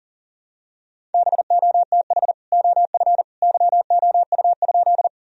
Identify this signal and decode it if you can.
Morse code, heard as a keyed tone, about 40 wpm, 700 Hz. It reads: BOTH OF YOU?
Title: Sentences from Top 1000 Words - Review Part 1 - 40wpm